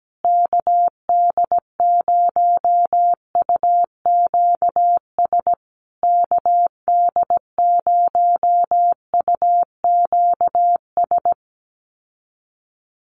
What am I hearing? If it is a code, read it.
Morse code, heard as a keyed tone, about 17 wpm, 700 Hz. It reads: KD0UQS KD0UQS
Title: USA Call Signs Repeated 3x 17wpm